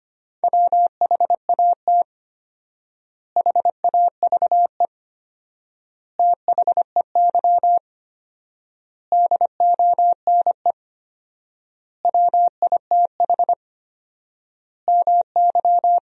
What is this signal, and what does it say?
Morse code, heard as a keyed tone, about 25 wpm, 700 Hz. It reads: WHAT HAVE THEY DONE WITH MY
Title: 4x Word-Spacing - Sentences from Top 500 Words - 25wpm